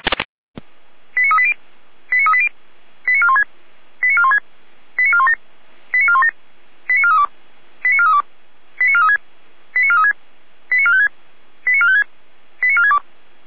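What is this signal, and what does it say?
Five-tone selective call (ZVEI), heard as quick runs of five tones, tones of about 70 ms. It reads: 89380 89380 89416 89416 89417 89417 89432 89432 89436 89436 89456 89456 89462
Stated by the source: ZVEI five-tone selective call, code 89401, received via SDR.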